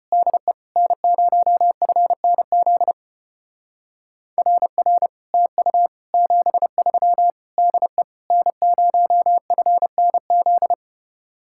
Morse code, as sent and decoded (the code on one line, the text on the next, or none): -.. / -. ----- ..-. -. --.. / .-. .-. / - ..- / --... ...-- / -.. / -. ----- ..-. -. --..
DE N0FNZ RR TU 73 DE N0FNZ